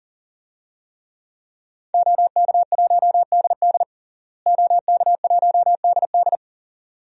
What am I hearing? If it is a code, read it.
Morse code, heard as a keyed tone, about 40 wpm, 700 Hz. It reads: OK1DD OK1DD
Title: DXCC-100 Call Signs - 5 Characters - Repeated 2x - 40wpm